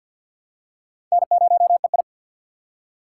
Morse code, heard as a keyed tone, about 52 wpm, 700 Hz. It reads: N0EI
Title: USA Call Signs - 4 Characters - 3x Character Spacing - 50wpm